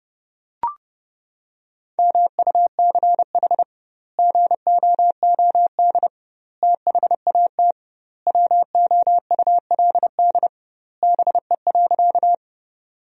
Morse code, heard as a keyed tone, about 30 wpm, 700 Hz. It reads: MUCH GOOD THAT WOULD BE.
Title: 2x Word-Spacing - Sentences from Top 200 Words - 30wpm